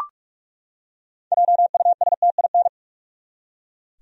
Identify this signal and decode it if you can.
Morse code, heard as a keyed tone, about 46 wpm, 700 Hz. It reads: JUSTIN
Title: USA Common Names - 45wpm